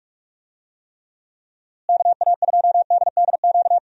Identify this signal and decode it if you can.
Morse code, heard as a keyed tone, about 41 wpm, 700 Hz. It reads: KA2DDQ
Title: USA Call Signs - 6 Characters - Repeated 3x - 45wpm